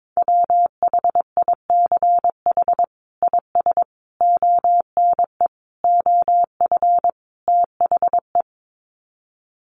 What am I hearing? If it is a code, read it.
Morse code, heard as a keyed tone, about 22 wpm, 700 Hz. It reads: WHICH IS ONE OF THE
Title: Sets of 5 Words Encore 22wpm